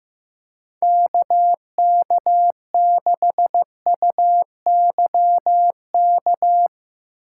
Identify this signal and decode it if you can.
Morse code, heard as a keyed tone, about 15 wpm, 700 Hz. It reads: KK6UYK
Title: USA Call Signs - 6 Characters - 8x Character Spacing - 15wpm